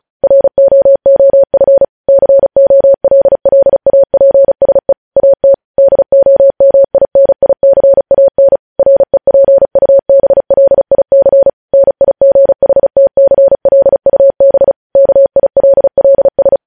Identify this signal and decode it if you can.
Morse code, heard as a keyed tone, about 36 wpm, 550 Hz. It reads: ROOF COLLAPSE AT DOMINICAN REPUBLIC NIGHTCLUB KILLS
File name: podcast / News headlines-1x-35x35